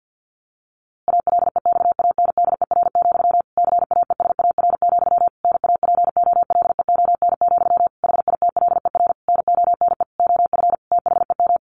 Morse code, heard as a keyed tone, about 50 wpm, 700 Hz. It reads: ALEXANDER, CAESAR, NAPOLEON, HITLER NONE OF THEM
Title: A Sound Of Thunder 50wpm